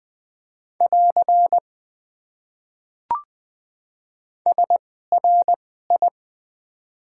Morse code, heard as a keyed tone, about 21 wpm, spaced as for 15 wpm, 700 Hz. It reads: <AR> SRI
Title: Top 90 QSO Elements - 2x Character Spacing - 20wpm